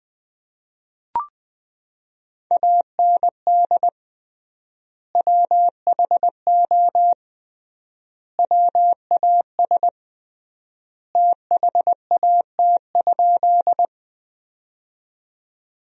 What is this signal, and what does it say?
Morse code, heard as a keyed tone, about 21 wpm, 700 Hz. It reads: AND WHO WAS THAT?
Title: 3x Word-Spacing - Sentences from Top 100 Words - 20wpm